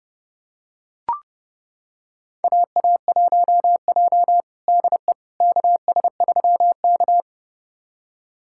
Morse code, heard as a keyed tone, about 32 wpm, 700 Hz. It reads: AA1J DE KS3K
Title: USA Callsign DE Callsign - 4 Characters - 30wpm